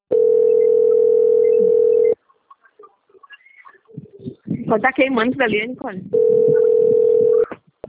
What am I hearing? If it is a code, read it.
Ringback tone (North America), 440 + 480 Hz, with speech.
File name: out- - - - - .53746